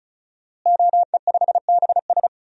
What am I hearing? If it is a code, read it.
Morse code, heard as a keyed tone, about 36 wpm, 700 Hz. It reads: OE5BS
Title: DXCC-100 Call Signs - 5 Characters - Repeated 3x - 35wpm